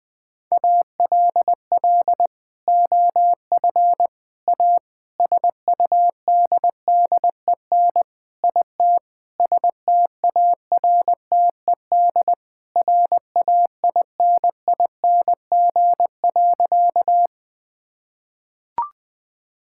Morse code, heard as a keyed tone, about 21 wpm, 700 Hz. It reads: ALL OF A SUDDEN IT STARTED RAINING.
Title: Sentences with 4-Word Phrases - Spoken Beginning - 20wpm